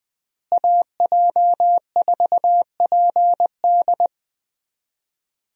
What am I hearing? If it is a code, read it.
Morse code, heard as a keyed tone, about 21 wpm, 700 Hz. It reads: AJ4PD
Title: USA Call Signs - 5 Characters - Repeated 2x - 20wpm